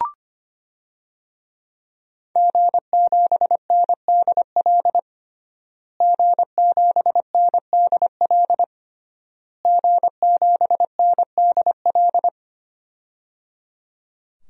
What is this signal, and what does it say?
Morse code, heard as a keyed tone, about 25 wpm, 700 Hz. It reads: G7NDL G7NDL G7NDL
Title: DXCC-100 Call Signs - 5 Characters - Repeated 3x - 25wpm